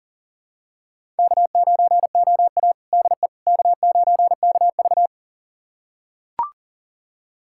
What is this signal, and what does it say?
Morse code, heard as a keyed tone, about 40 wpm, 700 Hz. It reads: K9OA DE K9KV
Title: USA Callsign DE Callsign - 4 Characters - 40wpm